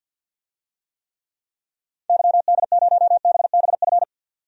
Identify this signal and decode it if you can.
Morse code, heard as a keyed tone, about 51 wpm, 700 Hz. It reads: YD0BBF
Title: DXCC-100 Call Signs - 6 Characters - 50wpm